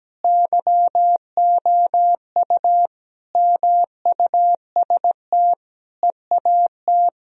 Morse code, heard as a keyed tone, about 17 wpm, 700 Hz. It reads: YOU MUST EAT
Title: Sentences from Top 1000 Words - Review Part 3 - 17wpm